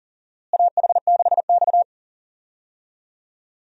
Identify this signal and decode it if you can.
Morse code, heard as a keyed tone, about 40 wpm, 700 Hz. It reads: AH6X
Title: USA Call Signs - 4 Characters - Repeated 2x - 40wpm